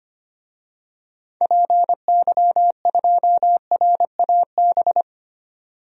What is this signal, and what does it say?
Morse code, heard as a keyed tone, about 25 wpm, 700 Hz. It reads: PY2RAB
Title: DXCC-100 Call Signs - 6 Characters - 25wpm